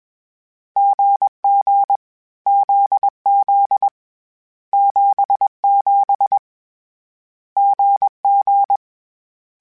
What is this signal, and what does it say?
Morse code, heard as a keyed tone, about 21 wpm, 800 Hz. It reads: GG ZZ 77 GG
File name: Short Runs / Short Runs-mp3s / GZ GZ7 @20